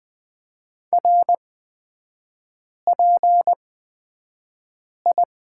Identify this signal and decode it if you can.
Morse code, heard as a keyed tone, about 20 wpm, spaced as for 5 wpm, 690 Hz. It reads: RPI
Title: 229 Lesson 37e - Letter x - Three Characters 20wpm